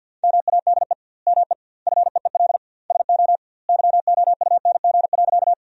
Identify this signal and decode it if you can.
Morse code, heard as a keyed tone, about 50 wpm, 700 Hz. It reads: MADE ME FEEL SO YOUNG.
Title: 2x Word-Spacing - Sentences from Top 400 Words - 50wpm